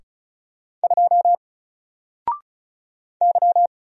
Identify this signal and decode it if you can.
Morse code, heard as a keyed tone, about 36 wpm, 700 Hz. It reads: 2 Y
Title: Single Character (Letter:Number:Punctuation:Prosign) 35wpm